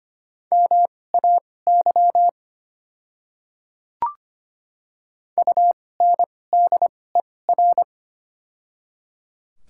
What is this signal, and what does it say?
Morse code, heard as a keyed tone, about 25 wpm, spaced as for 18 wpm, 690 Hz. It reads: MAY UNDER
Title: 2x Character Spacing - Top 500 Words - 25wpm